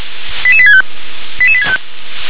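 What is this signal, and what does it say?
Five-tone selective call (ZVEI), heard as quick runs of five tones, tones of about 70 ms. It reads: 88865 88865
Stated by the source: Five-tone ZVEI selective call with code 88839, received by SDR.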